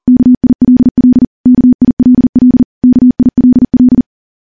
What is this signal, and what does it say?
Morse code, heard as a keyed tone, about 40 wpm, 250 Hz. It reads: KILL KILL KILL
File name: morse / morse-training-40WPM